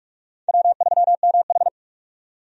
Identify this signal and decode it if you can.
Morse code, heard as a keyed tone, about 45 wpm, 700 Hz. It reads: W3MH
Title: USA Call Signs - 4 Characters - 45wpm